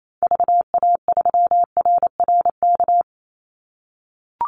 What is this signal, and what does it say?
Morse code, heard as a keyed tone, about 28 wpm, 700 Hz. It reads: VA3RRK
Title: Call Signs - 28wpm